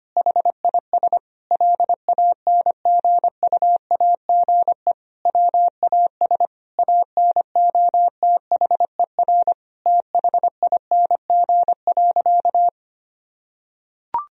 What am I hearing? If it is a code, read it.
Morse code, heard as a keyed tone, about 25 wpm, 700 Hz. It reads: HIS LANGUAGE WAS ANOTHER THING.
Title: Sentences from Top 1000 Words - Review Part 3 - 25wpm